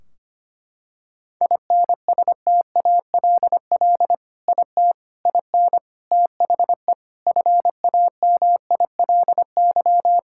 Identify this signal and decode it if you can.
Morse code, heard as a keyed tone, about 25 wpm, 700 Hz. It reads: INSTALL IT IN THE FAMILY
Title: Sentences from Top 1600 Words 25wpm